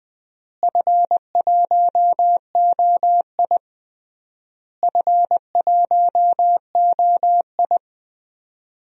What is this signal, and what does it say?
Morse code, heard as a keyed tone, about 20 wpm, 690 Hz. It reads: F1OI F1OI
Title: DXCC-100 Call Signs - 4 Characters - Repeated 3x - 20wpm